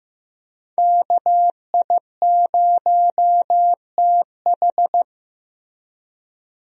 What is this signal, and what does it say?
Morse code, heard as a keyed tone, about 15 wpm, 700 Hz. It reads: KI0TH
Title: USA Call Signs - 5 Characters - Repeated 3x - 15wpm